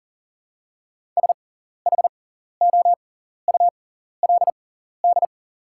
Morse code, heard as a keyed tone, about 40 wpm, spaced as for 14 wpm, 700 Hz. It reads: SHOULD
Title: 6x Character Spacing - Top 300 Words - 40wpm